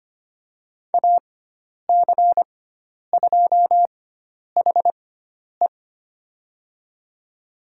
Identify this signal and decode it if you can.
Morse code, heard as a keyed tone, about 25 wpm, spaced as for 10 wpm, 700 Hz. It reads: AC2HE
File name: Lesson 16g - Letter c - Call Signs 25wpm